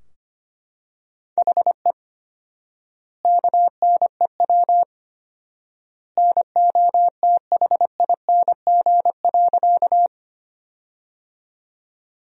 Morse code, heard as a keyed tone, about 25 wpm, 700 Hz. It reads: HE KNEW NOTHING.
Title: 4x Word-Spacing - Sentences from Top 500 Words - 25wpm